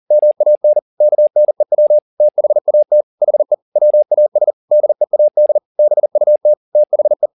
Morse code, heard as a keyed tone, about 40 wpm, 600 Hz. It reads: MAN KNEW THAT HE WAS DEAD BUT THE
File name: the old man and the sea 2 40wpm